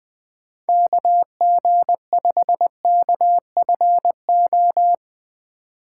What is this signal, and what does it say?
Morse code, heard as a keyed tone, about 20 wpm, 700 Hz. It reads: KG5KFO
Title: USA Call Signs - 6 Characters - 3x Character Spacing - 20wpm